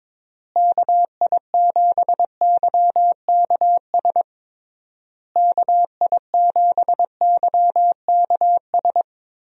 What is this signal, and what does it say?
Morse code, heard as a keyed tone, about 22 wpm, 700 Hz. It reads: KI7YKS KI7YKS
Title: USA Call Signs - 6 Characters - Repeated 3x - 22wpm